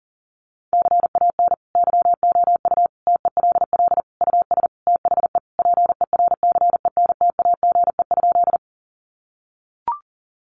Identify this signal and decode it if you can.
Morse code, heard as a keyed tone, about 40 wpm, 700 Hz. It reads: CAN YOU TELL US THE PERCENTAGE?
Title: Sentences from Top 1300 Words 40wpm